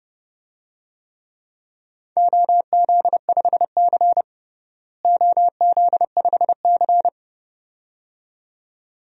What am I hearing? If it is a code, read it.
Morse code, heard as a keyed tone, about 30 wpm, 700 Hz. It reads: OZ5C OZ5C
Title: DXCC-100 Call Signs - 4 Characters - Repeated 2x - 30wpm